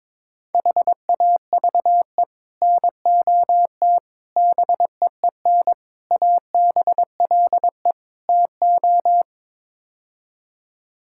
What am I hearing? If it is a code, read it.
Morse code, heard as a keyed tone, about 22 wpm, 700 Hz. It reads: HAVE NOT BEEN ABLE TO
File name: practice / 22wpm / Build A Phrase - 5-Word Phrases - 22wpm